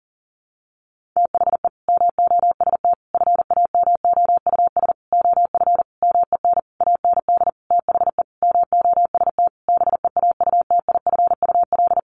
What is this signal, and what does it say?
Morse code, heard as a keyed tone, about 40 wpm, 700 Hz. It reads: THE MOST FAMOUS OF MEN AND THE MOST BEAUTIFUL
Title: Sentences from Top 1500 Words 40wpm